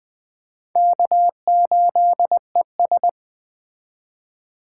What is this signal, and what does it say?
Morse code, heard as a keyed tone, about 21 wpm, 700 Hz. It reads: K8ES